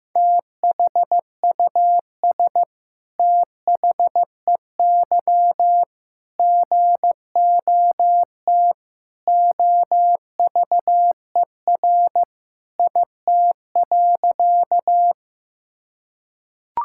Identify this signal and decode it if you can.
Morse code, heard as a keyed tone, about 15 wpm, 700 Hz. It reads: THUS THEY GOT OVER IT.
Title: Sentences from Top 1000 Words - Review Part 5 - 15wpm